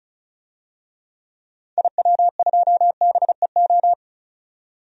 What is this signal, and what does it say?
Morse code, heard as a keyed tone, about 36 wpm, 700 Hz. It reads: IW2BEO
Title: DXCC-100 Call Signs - 6 Characters - 35wpm